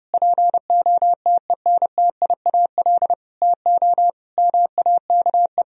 Morse code, heard as a keyed tone, about 32 wpm, 700 Hz. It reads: POTENTIAL TO MAKE
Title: Sentences from Top 1000 Words - Review Part 4 - 30wpm